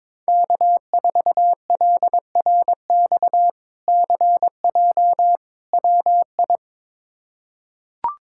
Morse code, heard as a keyed tone, about 22 wpm, 700 Hz. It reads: K4LRX CJ WI
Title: K1USN SST QSO Practice - 22wpm